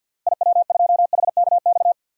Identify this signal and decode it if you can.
Morse code, heard as a keyed tone, about 49 wpm, 700 Hz. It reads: IW2HKX
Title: DXCC-100 Call Signs - 6 Characters - Repeated 3x - 50wpm